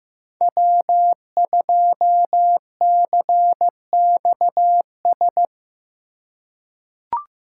Morse code, heard as a keyed tone, about 15 wpm, 700 Hz. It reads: W2CXS
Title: Code Groups - 5 Letter:Numbers - 15wpm